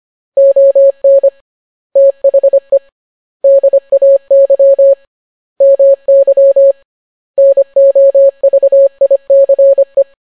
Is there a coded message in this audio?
Yes: Morse code from 0.4 to 10.0 s, heard as a keyed tone, about 25 wpm, 550 Hz. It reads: ON THE DAY MY NOVICE